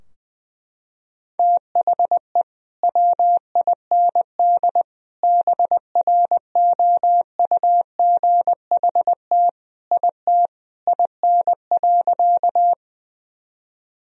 Morse code, heard as a keyed tone, about 21 wpm, 700 Hz. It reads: THE WIND BROUGHT IT IN.
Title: Sentences from Top 1000 Words - Review Part 2 - 20wpm